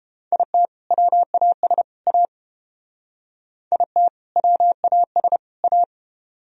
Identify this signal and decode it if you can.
Morse code, heard as a keyed tone, about 30 wpm, 700 Hz. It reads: IT WAS A IT WAS A